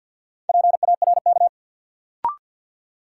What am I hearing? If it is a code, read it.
Morse code, heard as a keyed tone, about 49 wpm, 700 Hz. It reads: PARK
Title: Sentences from Top 1300 Words - New Word Drill - 50wpm